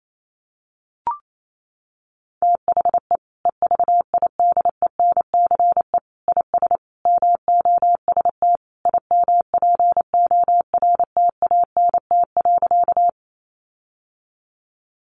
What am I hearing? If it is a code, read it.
Morse code, heard as a keyed tone, about 28 wpm, 700 Hz. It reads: THE EVIDENCE IS MOST IMPORTANT.
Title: Sentences from Top 1300 Words 28wpm